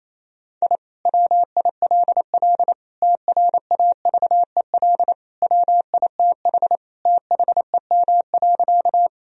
Morse code, heard as a keyed tone, about 28 wpm, 700 Hz. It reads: I WILL TRAVEL WITH THEM.
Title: Sentences from Top 1000 Words - Review Part 2 - 28wpm